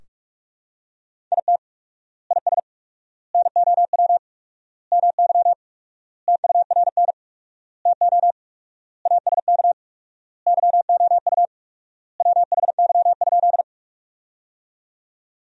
Morse code, heard as a keyed tone, about 43 wpm, 700 Hz. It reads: IT IS NOW MY TURN TO ASK YOU WHY?
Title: 4x Word-Spacing - Sentences from Top 200 Words - 45wpm